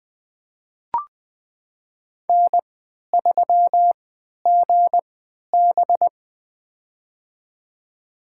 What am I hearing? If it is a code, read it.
Morse code, heard as a keyed tone, about 20 wpm, spaced as for 11 wpm, 700 Hz. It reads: N3GB
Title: USA Call Signs - 4 Characters - 3x Character Spacing - 20wpm